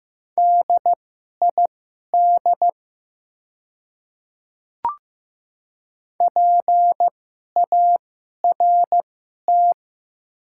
Morse code, heard as a keyed tone, about 15 wpm, spaced as for 11 wpm, 700 Hz. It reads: DID PART
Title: 2x Character Spacing - Top 200 Words - 15wpm